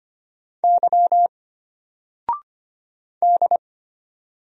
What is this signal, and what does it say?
Morse code, heard as a keyed tone, about 25 wpm, 700 Hz. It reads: Y D
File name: Single-Letter-Number-25wpm